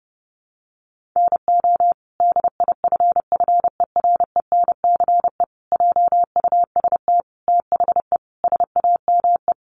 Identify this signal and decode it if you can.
Morse code, heard as a keyed tone, about 30 wpm, 700 Hz. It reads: NO DIFFERENCE JUST THE SAME
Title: Sentences from Top 800 Words 30wpm